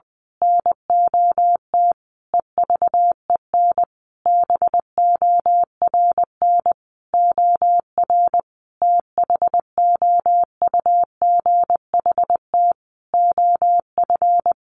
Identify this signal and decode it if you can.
Morse code, heard as a keyed tone, about 20 wpm, 700 Hz. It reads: NOT EVEN BORN OR THOUGHT OF
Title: A Sound Of Thunder 20wpm